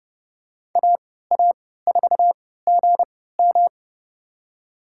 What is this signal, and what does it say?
Morse code, heard as a keyed tone, about 30 wpm, spaced as for 17 wpm, 700 Hz. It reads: AA4GM
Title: USA Call Signs - 5 Characters - 3x Character Spacing - 30wpm